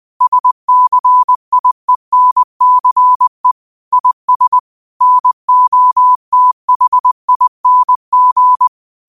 Morse code, heard as a keyed tone, about 20 wpm, 1000 Hz. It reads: SCIENCE IS NOTHING